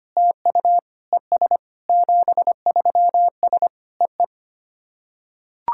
Morse code, heard as a keyed tone, about 25 wpm, 700 Hz. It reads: TU ES 73S EE
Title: SOTA QSO Practice 25wpm